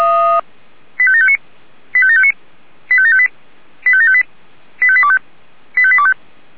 Five-tone selective call (ZVEI), heard as quick runs of five tones, tones of about 70 ms. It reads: 85759 85759 85758 85758 85825 85825